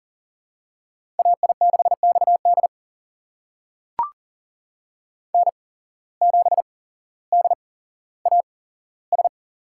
Morse code, heard as a keyed tone, about 39 wpm, spaced as for 17 wpm, 700 Hz. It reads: AI6XD N7DAS
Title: USA Call Signs - 5 Characters - 8x Character Spacing - 40wpm